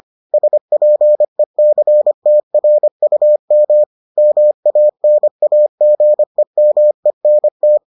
Morse code, heard as a keyed tone, about 25 wpm, 600 Hz. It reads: SPECTRUM MANAGEMENT